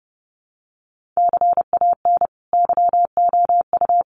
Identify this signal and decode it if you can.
Morse code, heard as a keyed tone, about 30 wpm, 700 Hz. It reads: CAN YOU
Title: Sentences from Top 1500 Words 30wpm